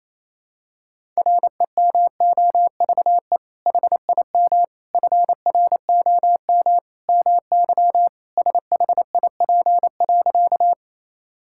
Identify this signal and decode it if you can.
Morse code, heard as a keyed tone, about 28 wpm, 700 Hz. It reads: REMOVE HIM FROM MY SHIP.
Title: Sentences from Top 1000 Words - Review Part 5 - 28wpm